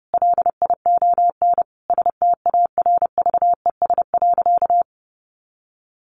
Morse code, heard as a keyed tone, about 30 wpm, 700 Hz. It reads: LION STARVES.
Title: A Sound Of Thunder 30wpm